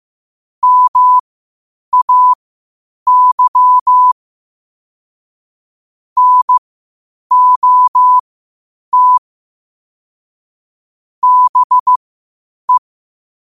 Morse code, heard as a keyed tone, about 15 wpm, spaced as for 8 wpm, 1000 Hz. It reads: MAY NOT BE
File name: qotdfiles / 8wpm